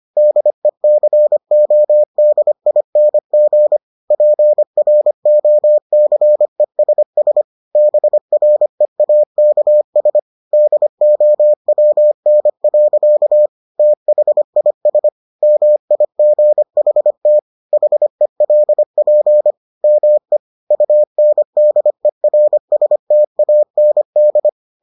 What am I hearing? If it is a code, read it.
Morse code, heard as a keyed tone, about 25 wpm, 600 Hz. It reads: DECODING PROCESS BREAKS DOWN. THIS MIGHT HELP ME UNDERSTAND